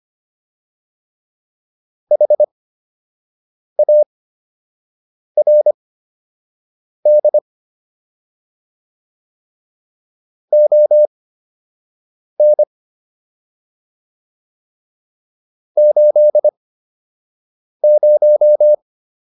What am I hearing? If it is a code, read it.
Morse code, heard as a keyed tone, about 25 wpm, spaced as for 6 wpm, 600 Hz. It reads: HARD ON 80